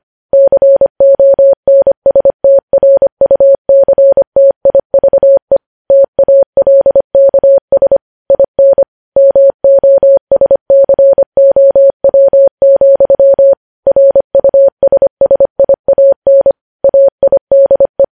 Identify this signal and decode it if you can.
Morse code, heard as a keyed tone, about 25 wpm, 550 Hz. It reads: CONSTRUCTIVE TALKS IN MOSCOW, RUSSIAN AIDE
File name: podcast / News headlines-1x-25x25